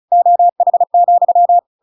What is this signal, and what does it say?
Morse code, heard as a keyed tone, about 35 wpm, 700 Hz. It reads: OH,